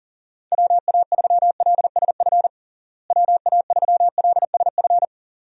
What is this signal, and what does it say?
Morse code, heard as a keyed tone, about 40 wpm, 700 Hz. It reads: WA3LSF WA3LSF
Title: USA Call Signs - 6 Characters - Repeated 2x - 40wpm